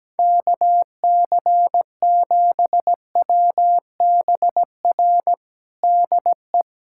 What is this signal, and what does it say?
Morse code, heard as a keyed tone, about 17 wpm, 700 Hz. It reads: KC7WBR DE
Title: USA Callsign DE Callsign - 6 Characters - 17wpm